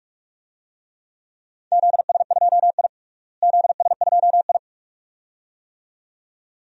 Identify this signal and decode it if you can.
Morse code, heard as a keyed tone, about 46 wpm, 700 Hz. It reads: ZS2I ZS2I
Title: DXCC-100 Call Signs - 4 Characters - Repeated 2x - 45wpm